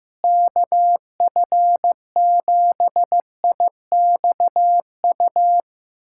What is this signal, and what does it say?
Morse code, heard as a keyed tone, about 15 wpm, 700 Hz. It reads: KF7IXU
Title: USA Callsign DE Callsign - 6 Characters - 15wpm